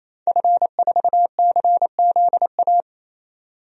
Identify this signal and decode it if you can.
Morse code, heard as a keyed tone, about 28 wpm, 700 Hz. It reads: F4CZA
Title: DXCC-100 Call Signs - 5 Characters - 28wpm